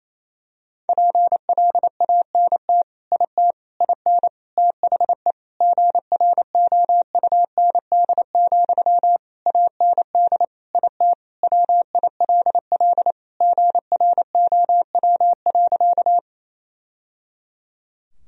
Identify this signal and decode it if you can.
Morse code, heard as a keyed tone, about 28 wpm, 700 Hz. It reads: PLANT IT IN THE GROUND, AND IT WILL GROW.
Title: Sentences from Top 1000 Words - Review Part 2 - 28wpm